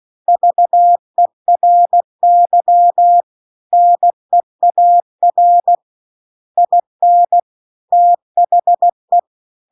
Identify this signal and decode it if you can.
Morse code, heard as a keyed tone, about 16 wpm, 700 Hz. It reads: VERY NEAR IN THE